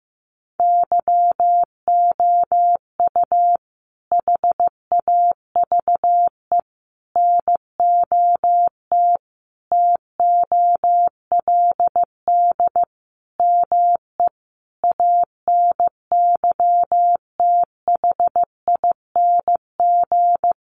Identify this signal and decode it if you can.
Morse code, heard as a keyed tone, about 15 wpm, 700 Hz. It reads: YOU HAVE NOT TOLD ME ANYTHING
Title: Sentences from Top 1000 Words 15wpm